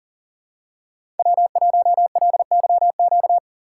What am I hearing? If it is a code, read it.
Morse code, heard as a keyed tone, about 42 wpm, 700 Hz. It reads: W1LYQ
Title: USA Call Signs - 5 Characters - 6x Character Spacing - 40wpm